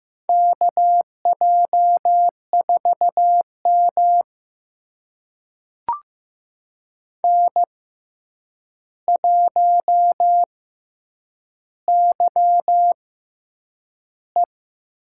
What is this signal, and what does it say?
Morse code, heard as a keyed tone, about 15 wpm, spaced as for 8 wpm, 700 Hz. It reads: KJ4M N1YE
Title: USA Call Signs - 4 Characters - 6x Character Spacing - 15wpm